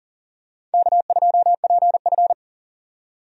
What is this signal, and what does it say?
Morse code, heard as a keyed tone, about 40 wpm, 700 Hz. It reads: K2PF
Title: USA Call Signs - 4 Characters - 40wpm